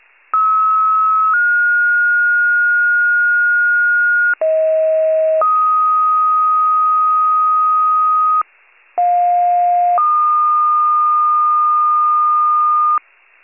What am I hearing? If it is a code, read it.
Two-tone page, heard as a one-second tone, then a three-second tone, about 1290 then 1430 Hz, about 640 then 1250 Hz, about 710 then 1220 Hz. It is heard from 0.3 to 13.0 s.
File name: pager/84 FIRE